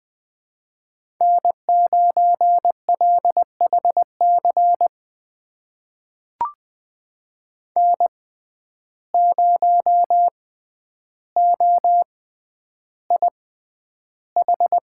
Morse code, heard as a keyed tone, about 20 wpm, spaced as for 10 wpm, 700 Hz. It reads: N9LHC N0OIH
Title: USA Call Signs - 5 Characters - 6x Character Spacing - 20wpm